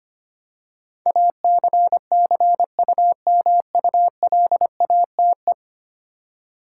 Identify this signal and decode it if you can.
Morse code, heard as a keyed tone, about 25 wpm, 700 Hz. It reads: ACCUMULATE
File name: 10 Letter Words - First 3 Letters Given - 25wpm